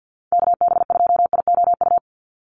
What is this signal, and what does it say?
Morse code, heard as a keyed tone, about 50 wpm, 700 Hz. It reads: KB2IOU
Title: USA Call Signs 50wpm